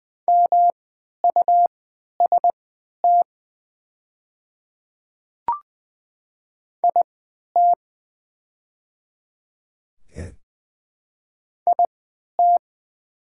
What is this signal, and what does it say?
Morse code, heard as a keyed tone, about 20 wpm, spaced as for 11 wpm, 700 Hz. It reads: MUST IT IT
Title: 3x Character Spacing - Top 200 Words - 20wpm